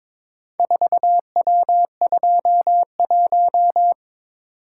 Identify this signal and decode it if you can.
Morse code, heard as a keyed tone, about 22 wpm, 700 Hz. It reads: 4W21
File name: practice / 22wpm / Four Letter-Number Combinations - 22wpm